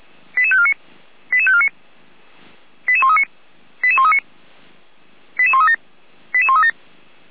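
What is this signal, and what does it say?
Five-tone selective call (ZVEI), heard as quick runs of five tones, tones of about 70 ms. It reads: 88549 88549 88139 88139 88137 88137
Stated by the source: ZVEI five-tone call received by SDR, from the Feuerwehr Orlamünde audio server.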